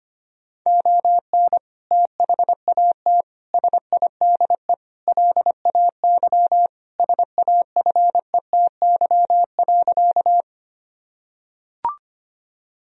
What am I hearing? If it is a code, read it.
Morse code, heard as a keyed tone, about 25 wpm, 700 Hz. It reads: ON THAT SIDE LAY SAFETY.
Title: Sentences from Top 1000 Words - Review Part 4 - 25wpm